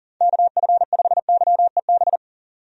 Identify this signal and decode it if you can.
Morse code, heard as a keyed tone, about 38 wpm, 700 Hz. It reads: KF5YEB
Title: USA Call Signs - 6 Characters - 40wpm